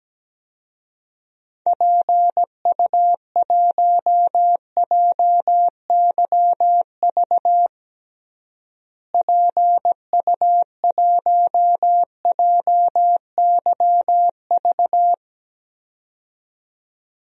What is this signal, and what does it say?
Morse code, heard as a keyed tone, about 17 wpm, 700 Hz. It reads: PU1JYV PU1JYV
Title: DXCC-100 Call Signs - 6 Characters - Repeated 2x - 17wpm